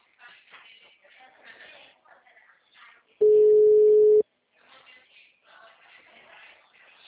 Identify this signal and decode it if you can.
Ringback tone (Europe, 425 Hz).
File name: out- -2020- - - .416